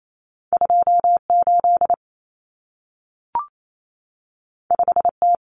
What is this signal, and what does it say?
Morse code, heard as a keyed tone, about 28 wpm, 700 Hz. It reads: 28 5T
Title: Common Cut Numbers - 2-Digit 28wpm